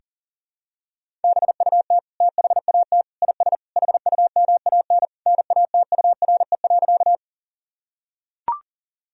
Morse code, heard as a keyed tone, about 40 wpm, 700 Hz. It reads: BUT THAT IS HUMAN NATURE.
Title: Sentences from Top 1000 Words - Review Part 3 - 40wpm